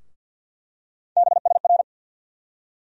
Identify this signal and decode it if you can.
Morse code, heard as a keyed tone, about 48 wpm, 700 Hz. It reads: BSR
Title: USA Call Sign Postfixes - 3 Letters 50wpm